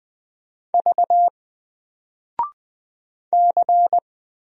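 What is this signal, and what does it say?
Morse code, heard as a keyed tone, about 20 wpm, 700 Hz. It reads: V C
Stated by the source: Single letter and number Morse drill at 20 wpm, from N4LCD.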